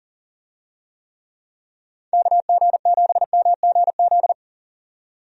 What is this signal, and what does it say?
Morse code, heard as a keyed tone, about 40 wpm, 700 Hz. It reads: KG7MGZ
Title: USA Call Signs - 6 Characters - 40wpm